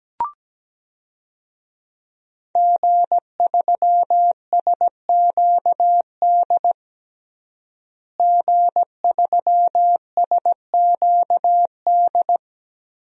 Morse code, heard as a keyed tone, about 17 wpm, 700 Hz. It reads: G3SQD G3SQD
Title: DXCC-100 Call Signs - 5 Characters - Repeated 3x - 17wpm